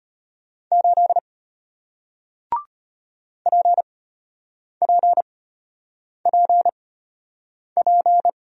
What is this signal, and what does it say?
Morse code, heard as a keyed tone, about 33 wpm, spaced as for 8 wpm, 700 Hz. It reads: 8 PPPP